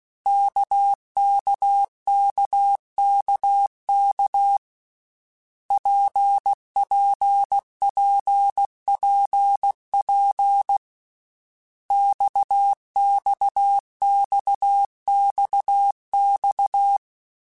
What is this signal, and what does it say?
Morse code, heard as a keyed tone, about 16 wpm, 800 Hz. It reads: KKKKK PPPPP XXXXX
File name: Basic Letter Groups By7Groups / RKPX / RKPX@13